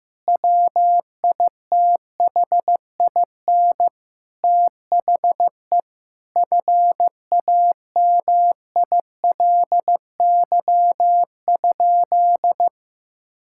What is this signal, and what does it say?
Morse code, heard as a keyed tone, about 16 wpm, 700 Hz. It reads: WITHIN THE FAMILY?
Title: Sentences from Top 1600 Words 15wpm